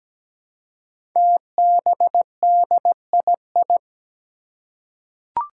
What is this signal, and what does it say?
Morse code, heard as a keyed tone, about 17 wpm, 700 Hz. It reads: TBDII
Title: Five Challenging Letter-Number Combinations 17wpm